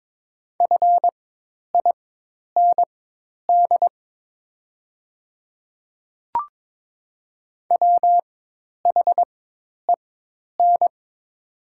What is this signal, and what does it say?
Morse code, heard as a keyed tone, about 22 wpm, spaced as for 10 wpm, 700 Hz. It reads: FIND WHEN
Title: 4x Character Spacing - Top 100 Words - 22wpm